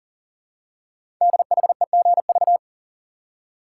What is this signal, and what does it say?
Morse code, heard as a keyed tone, about 40 wpm, 700 Hz. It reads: DHEGV
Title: Code Groups - 5 Letters - 40wpm